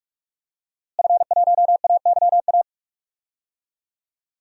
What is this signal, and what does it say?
Morse code, heard as a keyed tone, about 47 wpm, 700 Hz. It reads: F1AYA
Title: DXCC-100 Call Signs - 5 Characters - Repeated 2x - 45wpm